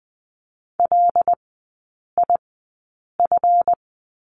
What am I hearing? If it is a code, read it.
Morse code, heard as a keyed tone, about 20 wpm, spaced as for 8 wpm, 700 Hz. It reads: LIF
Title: 138 Lesson 23e - Letter f - Three Characters 20wpm